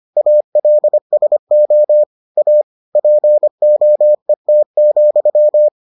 Morse code, heard as a keyed tone, about 25 wpm, 600 Hz. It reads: ALSO A POET,